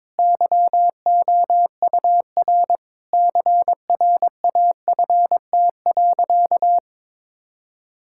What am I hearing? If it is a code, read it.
Morse code, heard as a keyed tone, about 22 wpm, 700 Hz. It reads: YOUR CRAFT.
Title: Sentences from Top 1000 Words - Review Part 6 - 22wpm